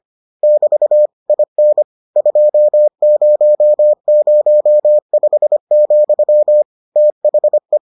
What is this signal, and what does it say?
Morse code, heard as a keyed tone, about 25 wpm, 600 Hz. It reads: <BT> IN 2005, THE